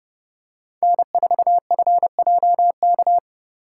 Morse code, heard as a keyed tone, about 30 wpm, 700 Hz. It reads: N4FJK